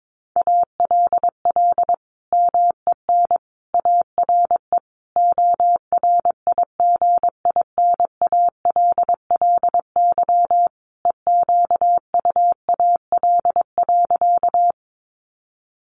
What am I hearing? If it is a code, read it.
Morse code, heard as a keyed tone, about 22 wpm, 700 Hz. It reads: ALL MEN ARE ORIGINALLY EQUAL.
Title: Sentences from Top 1500 Words 22wpm